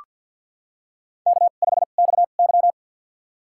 Morse code, heard as a keyed tone, about 53 wpm, spaced as for 37 wpm, 700 Hz. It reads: K5XY
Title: USA Call Signs - 4 Characters - 2x Character Spacing - 50wpm